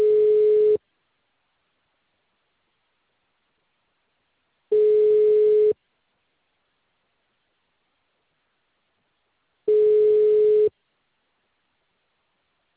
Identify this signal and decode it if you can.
Ringback tone (Europe, 425 Hz).